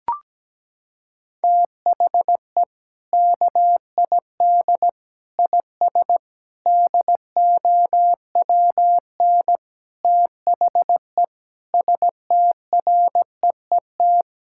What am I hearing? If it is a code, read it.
Morse code, heard as a keyed tone, about 17 wpm, 700 Hz. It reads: THE KID IS DOWN THE STREET
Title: Sentences from Top 1600 Words 17wpm